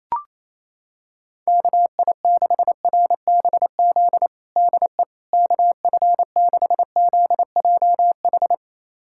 Morse code, heard as a keyed tone, about 28 wpm, 700 Hz. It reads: KI6RBZ DE KF6ZJH
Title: USA Callsign DE Callsign - 6 Characters - 28wpm